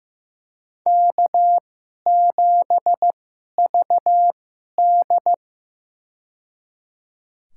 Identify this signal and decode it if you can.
Morse code, heard as a keyed tone, about 15 wpm, spaced as for 11 wpm, 700 Hz. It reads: K7VD
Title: USA Call Signs - 4 Characters - 2x Character Spacing - 15wpm